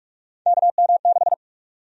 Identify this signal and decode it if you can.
Morse code, heard as a keyed tone, about 43 wpm, 700 Hz. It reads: KM6
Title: USA Call Sign Prefixes - 45wpm